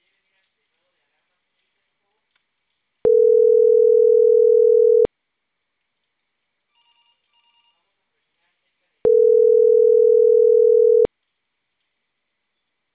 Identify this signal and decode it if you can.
Ringback tone (North America), 440 + 480 Hz.